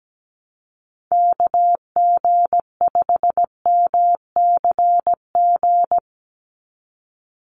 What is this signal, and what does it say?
Morse code, heard as a keyed tone, about 17 wpm, 700 Hz. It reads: KG5MCG
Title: USA Call Signs 17wpm